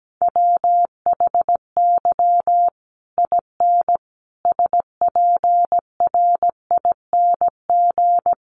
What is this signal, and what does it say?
Morse code, heard as a keyed tone, about 17 wpm, 700 Hz. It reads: WHY IN SPRING
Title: Sentences from Top 1200 Words 17wpm